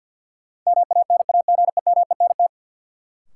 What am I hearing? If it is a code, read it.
Morse code, heard as a keyed tone, about 52 wpm, 700 Hz. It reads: MANAGEMENT
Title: Sentences from Top 1000 Words - Review Part 4 - New Word Drill - 50wpm